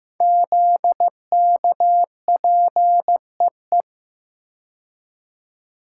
Morse code, heard as a keyed tone, about 15 wpm, 700 Hz. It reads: ZKPEE
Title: Code Groups - 5 Letters - 15wpm